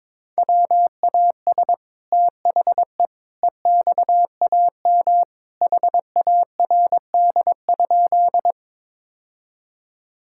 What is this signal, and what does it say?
Morse code, heard as a keyed tone, about 22 wpm, 700 Hz. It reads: WAS THE EXAM HARD?
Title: Sentences from Top 1000 Words - Review Part 5 - 22wpm